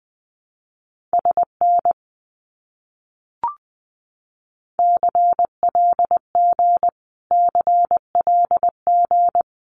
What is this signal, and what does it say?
Morse code, heard as a keyed tone, about 19 wpm, 700 Hz. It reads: SN CLG CLG
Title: Abbreviations Repeated 3x - 20wpm